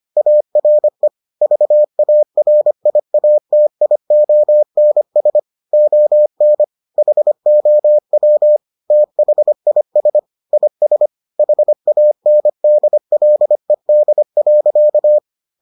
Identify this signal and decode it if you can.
Morse code, heard as a keyed tone, about 25 wpm, 600 Hz. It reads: ARE VARIATIONS ON HOW THIS IS HANDLED.